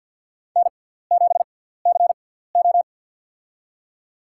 Morse code, heard as a keyed tone, about 51 wpm, spaced as for 17 wpm, 700 Hz. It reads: N7CO
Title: USA Call Signs - 4 Characters - 6x Character Spacing - 50wpm